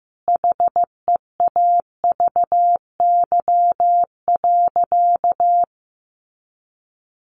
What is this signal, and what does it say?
Morse code, heard as a keyed tone, about 15 wpm, 700 Hz. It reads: HEAVY.
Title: Sentences from Top 1200 Words 15wpm